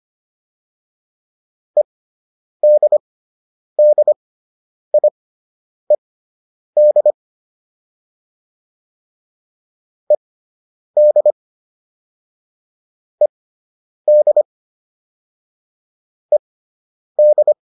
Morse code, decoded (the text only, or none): EDDIED ED ED ED